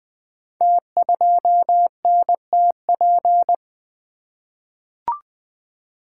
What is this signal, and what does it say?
Morse code, heard as a keyed tone, about 20 wpm, 700 Hz. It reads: T2NTP
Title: Code Groups - 5 Letter:Numbers - 20wpm